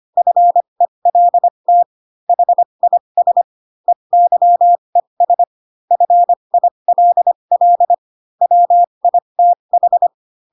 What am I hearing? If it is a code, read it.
Morse code, heard as a keyed tone, about 25 wpm, spaced as for 22 wpm, 700 Hz. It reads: FELT HIS EYES FILL WITH